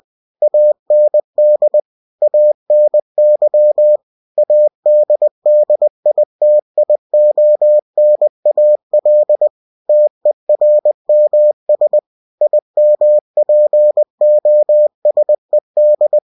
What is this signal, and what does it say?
Morse code, heard as a keyed tone, about 20 wpm, 600 Hz. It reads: AND ANY ADDITIONAL TERMS IMPOSED
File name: winnie-20wpm